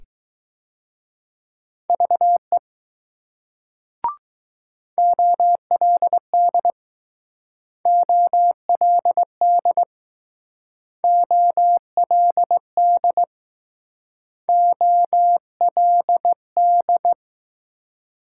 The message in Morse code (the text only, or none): VE OLD OLD OLD OLD